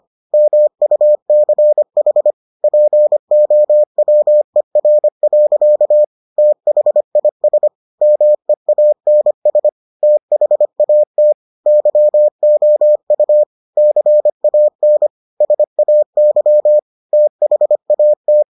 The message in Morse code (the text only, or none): MUCH POWER. THIS MEANS THAT YOU CAN SAY THAT